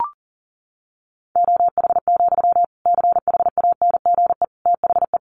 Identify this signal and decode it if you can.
Morse code, heard as a keyed tone, about 40 wpm, 700 Hz. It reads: OH, CHANGE THE
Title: Sentences from Top 800 Words 40wpm